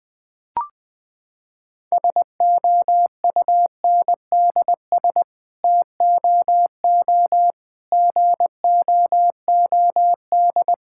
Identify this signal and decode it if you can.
Morse code, heard as a keyed tone, about 20 wpm, 700 Hz. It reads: SOUNDS TOO GOOD
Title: Sets of 9 Words - 20wpm